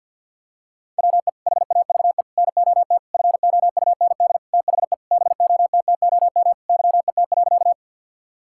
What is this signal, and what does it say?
Morse code, heard as a keyed tone, about 50 wpm, 700 Hz. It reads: WE HAVE NOT FOUND THE BOTTOM YET.
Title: Sentences from Top 1000 Words - Review Part 6 - 50wpm